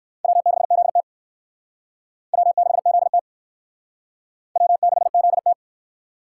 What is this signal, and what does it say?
Morse code, heard as a keyed tone, about 56 wpm, 700 Hz. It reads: W6ZT W6ZT W6ZT
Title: Speed Racing - USA 1x2 Call Signs 45wpm